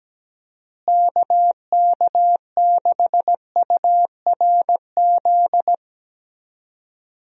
Morse code, heard as a keyed tone, about 17 wpm, 700 Hz. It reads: KK6URZ
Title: USA Call Signs - 6 Characters - Repeated 3x - 17wpm